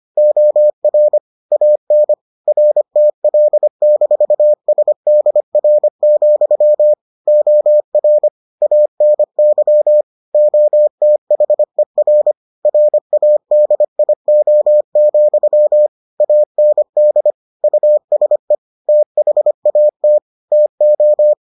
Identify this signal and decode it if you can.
Morse code, heard as a keyed tone, about 25 wpm, 600 Hz. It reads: OR AN RTL-SDR, OR ANY OTHER RADIO, AND USE THAT TO